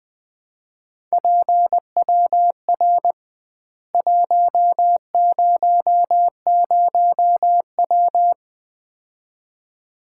Morse code, heard as a keyed tone, about 20 wpm, 700 Hz. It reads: PWR 100W
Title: Common QSO Phrases - 2x Word-Spacing - 20wpm